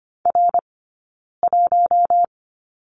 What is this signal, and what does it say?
Morse code, heard as a keyed tone, about 25 wpm, spaced as for 9 wpm, 690 Hz. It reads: R1
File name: course / 25:10wpm / 092 Lesson 16d - Letter c - Two Characters 25wpm